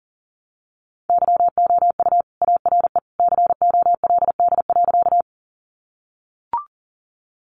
Morse code, heard as a keyed tone, about 40 wpm, 700 Hz. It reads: YOU ARE COLD.
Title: Sentences from Top 400 Words 40wpm